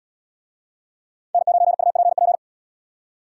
Morse code, heard as a keyed tone, about 72 wpm, 700 Hz. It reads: N9UPP